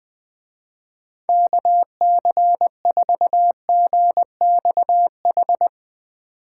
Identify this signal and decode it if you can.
Morse code, heard as a keyed tone, about 21 wpm, 700 Hz. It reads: KC4GXH